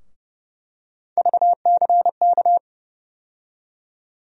Morse code, heard as a keyed tone, about 31 wpm, 700 Hz. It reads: VCK
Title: USA Call Sign Postfixes - 3 Letters 30wpm